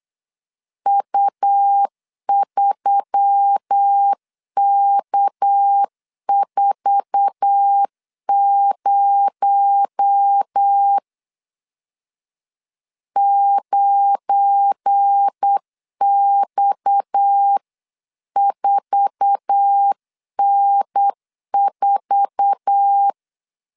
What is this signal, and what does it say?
Morse code, heard as a keyed tone, about 8 wpm, 800 Hz. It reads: U3K40 9X4N4
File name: LEZIONI / 8 YJ45 / 8 YJ4 8wpm 3246